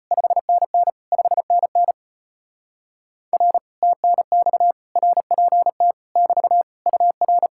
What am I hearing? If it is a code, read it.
Morse code, heard as a keyed tone, about 33 wpm, 700 Hz. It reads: 5NN 5NN R TNX RPT <BT> UR